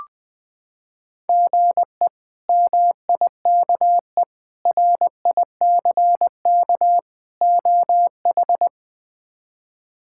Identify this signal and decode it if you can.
Morse code, heard as a keyed tone, about 20 wpm, 700 Hz. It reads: GE MIKE RICK OH
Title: K1USN SST QSO Practice - 20wpm